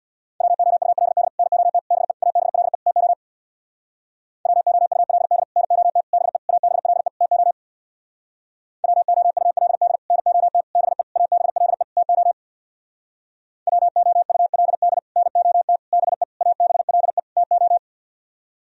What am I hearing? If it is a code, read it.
Morse code, heard as a keyed tone, about 64 wpm, 700 Hz. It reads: WOULD NOT BE ABLE TO WOULD NOT BE ABLE TO WOULD NOT BE ABLE TO WOULD NOT BE ABLE TO